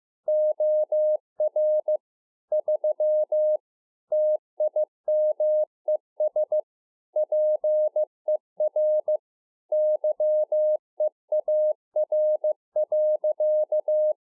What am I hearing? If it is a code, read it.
Morse code, heard as a keyed tone, about 15 wpm, 610 Hz. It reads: OR 3 TIMES PER YEAR.
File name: 2011-04-mp3 / YAMA 7K1CPT 15WPM